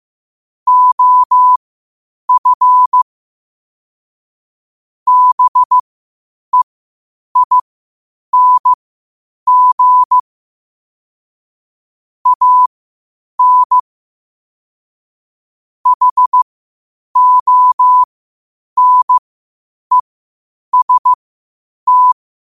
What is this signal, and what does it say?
Morse code, heard as a keyed tone, about 15 wpm, spaced as for 8 wpm, 1000 Hz. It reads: OF BEING AN HONEST
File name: qotdfiles / 8wpm